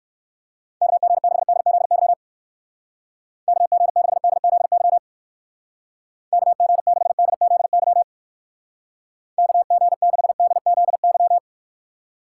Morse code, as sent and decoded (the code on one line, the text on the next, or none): -.- --. -.... -.. --.. -.-- / -.- --. -.... -.. --.. -.-- / -.- --. -.... -.. --.. -.-- / -.- --. -.... -.. --.. -.--
KG6DZY KG6DZY KG6DZY KG6DZY